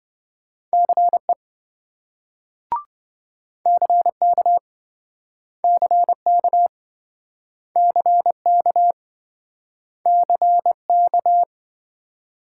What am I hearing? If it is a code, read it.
Morse code, heard as a keyed tone, about 27 wpm, 700 Hz. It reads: CE CK CK CK CK